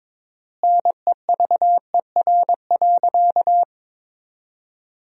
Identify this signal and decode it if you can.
Morse code, heard as a keyed tone, about 22 wpm, 700 Hz. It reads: NEVER.
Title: 2x Word-Spacing - Sentences from Top 300 Words - 22wpm